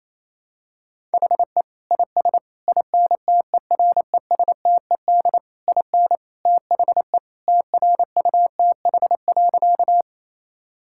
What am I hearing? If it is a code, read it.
Morse code, heard as a keyed tone, about 28 wpm, 700 Hz. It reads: HE IS INTERESTED IN THE TRUTH.
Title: Sentences from Top 1000 Words - Review Part 6 - 28wpm